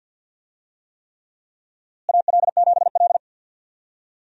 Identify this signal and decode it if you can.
Morse code, heard as a keyed tone, about 48 wpm, 700 Hz. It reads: AL7L
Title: USA Call Signs - 4 Characters - Repeated 3x - 50wpm